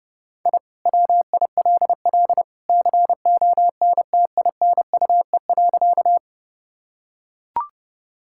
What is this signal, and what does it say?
Morse code, heard as a keyed tone, about 32 wpm, 700 Hz. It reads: I WILL CONTINUE.
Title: Sentences from Top 1000 Words - Review Part 6 - 30wpm